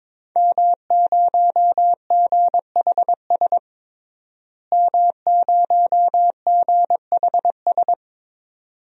Morse code, heard as a keyed tone, about 22 wpm, 700 Hz. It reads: M0GHS M0GHS
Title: DXCC-100 Call Signs - 5 Characters - Repeated 2x - 22wpm